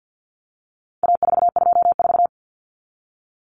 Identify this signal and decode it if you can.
Morse code, heard as a keyed tone, about 48 wpm, 700 Hz. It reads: A424
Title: Common Cut Numbers - 4-Digit 50wpm